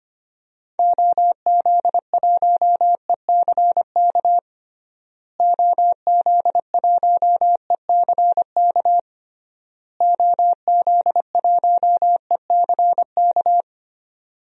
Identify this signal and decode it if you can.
Morse code, heard as a keyed tone, about 25 wpm, 700 Hz. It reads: OZ1ECK OZ1ECK OZ1ECK
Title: DXCC-100 Call Signs - 6 Characters - Repeated 3x - 25wpm